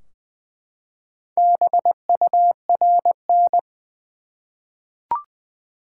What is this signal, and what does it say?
Morse code, heard as a keyed tone, about 20 wpm, 700 Hz. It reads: BURN
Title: Four Letter Words - 20wpm